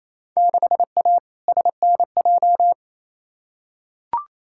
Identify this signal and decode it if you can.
Morse code, heard as a keyed tone, about 28 wpm, 700 Hz. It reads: 6A SNJ
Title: Field Day Exchanges Encore - 28wpm